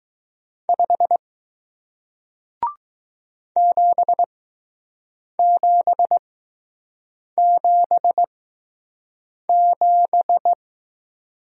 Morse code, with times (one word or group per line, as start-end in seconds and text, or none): 0.7-1.2 5
3.6-10.5 7777